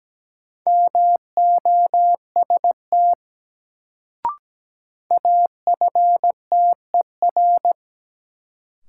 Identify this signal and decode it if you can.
Morse code, heard as a keyed tone, about 17 wpm, 700 Hz. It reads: MOST AFTER
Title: Sentences from Top 200 Words - New Word Drill - 17wpm